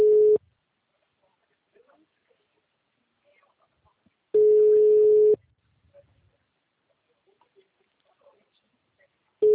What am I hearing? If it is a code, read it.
Ringback tone (Europe, 425 Hz).